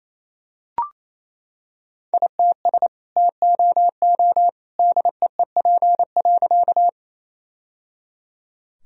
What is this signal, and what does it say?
Morse code, heard as a keyed tone, about 28 wpm, 700 Hz. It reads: ITS TOO DEEP.
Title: Sentences from Top 1000 Words - Review Part 3 - 28wpm